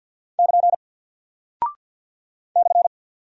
Morse code, heard as a keyed tone, about 48 wpm, 700 Hz. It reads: <KN> /
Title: Single Character (Letter:Number:Punctuation:Prosign) Alt 50wpm